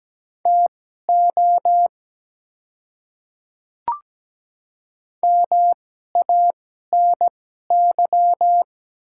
Morse code, heard as a keyed tone, about 17 wpm, spaced as for 12 wpm, 700 Hz. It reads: TO MANY
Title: 2x Character Spacing - Top 100 Words - 17wpm